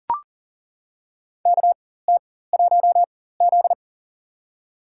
Morse code, heard as a keyed tone, about 40 wpm, spaced as for 19 wpm, 700 Hz. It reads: KT1Z
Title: USA Call Signs - 4 Characters - 4x Character Spacing - 40wpm